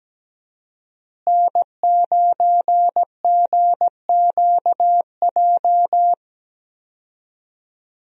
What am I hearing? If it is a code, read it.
Morse code, heard as a keyed tone, about 17 wpm, 700 Hz. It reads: N9GQJ
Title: USA Call Signs - 5 Characters - Repeated 3x - 17wpm